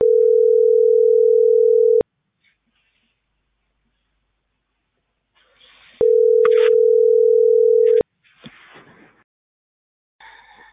Ringback tone (North America), 440 + 480 Hz.